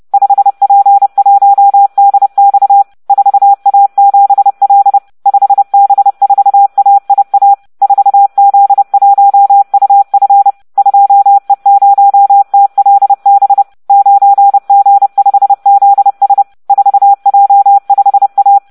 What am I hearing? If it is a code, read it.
Morse code, heard as a keyed tone, about 30 wpm, 800 Hz. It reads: 5P1DX 4A7L 5B4AIA 4Z1UF 2E0TLB 9G5ZS 4J5A